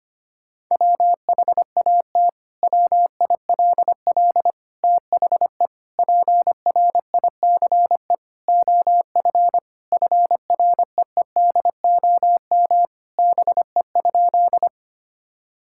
Morse code, heard as a keyed tone, about 25 wpm, 700 Hz. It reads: WHAT WILL THE PRICE OF FREEDOM BE?
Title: Sentences from Top 1000 Words - Review Part 5 - 25wpm